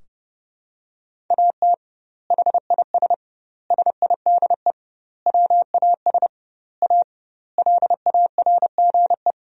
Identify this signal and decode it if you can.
Morse code, heard as a keyed tone, about 30 wpm, 700 Hz. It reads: AT HIS SIDE WAS A LARGE
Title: 2x Word-Spacing - Sentences from Top 500 Words - 30wpm